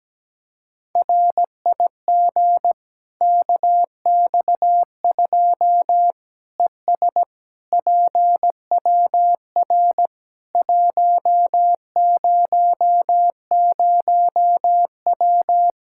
Morse code, heard as a keyed tone, about 17 wpm, 700 Hz. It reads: RIG KX2 ES PWR 100W
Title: K1USN SST QSO Practice 17wpm